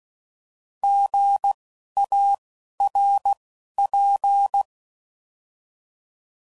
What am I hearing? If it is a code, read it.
Morse code, heard as a keyed tone, about 16 wpm, spaced as for 11 wpm, 800 Hz. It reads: GARP
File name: Prefixes / By First Letter